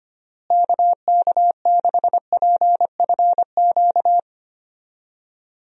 Morse code, heard as a keyed tone, about 25 wpm, 700 Hz. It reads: KK6PFQ
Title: USA Call Signs - 6 Characters - Repeated 3x - 25wpm